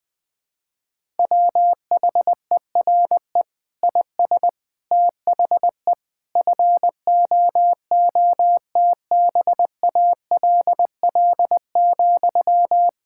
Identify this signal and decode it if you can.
Morse code, heard as a keyed tone, about 20 wpm, 700 Hz. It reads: WHERE IS THE FOOTBALL,
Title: Sentences from Top 1600 Words 20wpm